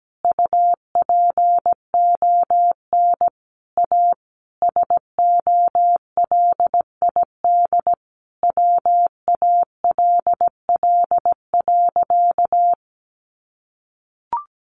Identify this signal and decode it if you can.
Morse code, heard as a keyed tone, about 17 wpm, 700 Hz. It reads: UPON A SOLID WALL.
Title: Sentences from Top 1500 Words 17wpm